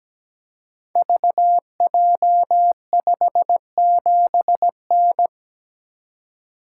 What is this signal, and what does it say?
Morse code, heard as a keyed tone, about 17 wpm, 700 Hz. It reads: VJ57N
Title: Code Groups - 5 Letter:Numbers - 17wpm